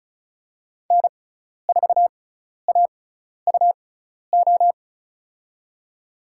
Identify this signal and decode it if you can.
Morse code, heard as a keyed tone, about 36 wpm, spaced as for 12 wpm, 700 Hz. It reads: N4AUO
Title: USA Call Signs - 5 Characters - 6x Character Spacing - 35wpm